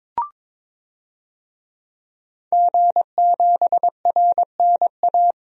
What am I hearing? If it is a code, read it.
Morse code, heard as a keyed tone, about 22 wpm, 700 Hz. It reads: G7RNA
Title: DXCC-100 Call Signs - 5 Characters - 22wpm